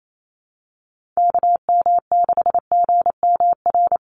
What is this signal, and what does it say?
Morse code, heard as a keyed tone, about 28 wpm, 700 Hz. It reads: KM6GMR
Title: USA Call Signs 28wpm